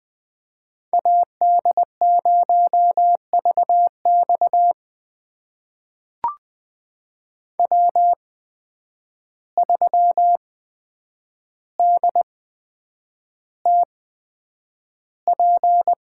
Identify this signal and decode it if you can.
Morse code, heard as a keyed tone, about 20 wpm, spaced as for 9 wpm, 700 Hz. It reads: AD0VX W3DTP
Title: USA Call Signs - 5 Characters - 8x Character Spacing - 20wpm